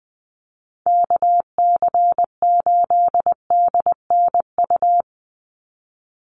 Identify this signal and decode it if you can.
Morse code, heard as a keyed tone, about 20 wpm, 700 Hz. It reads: KC8DNU
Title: USA Call Signs 20wpm